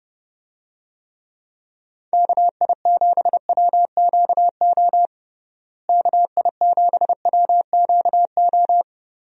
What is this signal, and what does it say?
Morse code, heard as a keyed tone, about 30 wpm, 700 Hz. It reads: KI7WQO KI7WQO
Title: USA Call Signs - 6 Characters - Repeated 3x - 30wpm